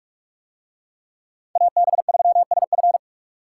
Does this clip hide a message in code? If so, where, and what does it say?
Yes: Morse code from 1.6 to 3.0 s, heard as a keyed tone, about 46 wpm, 700 Hz. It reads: AB3SF